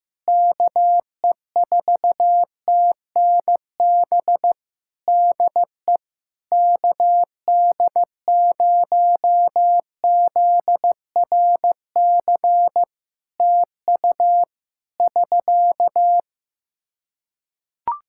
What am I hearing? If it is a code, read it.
Morse code, heard as a keyed tone, about 15 wpm, 700 Hz. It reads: KE4TNB DE KD0ZRC TU <SK>
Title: K1USN SST QSO Practice 15wpm